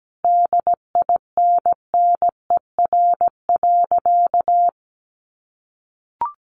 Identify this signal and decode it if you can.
Morse code, heard as a keyed tone, about 17 wpm, 700 Hz. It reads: DINNER.
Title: Easy Phrases 17wpm